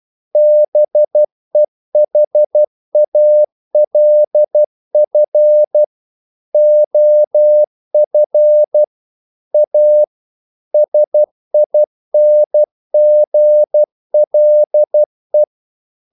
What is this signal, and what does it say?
Morse code, heard as a keyed tone, about 12 wpm, 600 Hz. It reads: BEHALF OF A SINGLE
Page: English quotations in CW (telegraphy - morse)